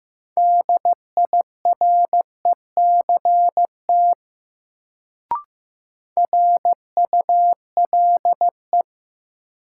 Morse code, heard as a keyed tone, about 15 wpm, 700 Hz. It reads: DIRECT RULE
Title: Sentences from Top 400 Words - New Word Drill - 15wpm